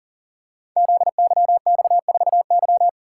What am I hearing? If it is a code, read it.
Morse code, heard as a keyed tone, about 40 wpm, 690 Hz. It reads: ZYX4Y
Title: Code Groups - 5 Letter:Numbers - 40wpm